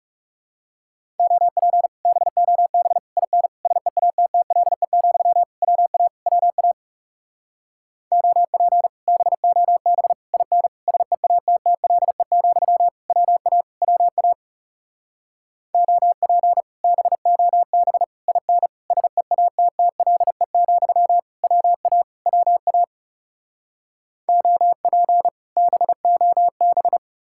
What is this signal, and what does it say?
Morse code, heard as a keyed tone, about 37 wpm, 700 Hz. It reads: OP BOB IN SEATTLE, WA WA OP BOB IN SEATTLE, WA WA OP BOB IN SEATTLE, WA WA OP BOB
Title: Speed-Racing Common QSO Phrases Encore - 30wpm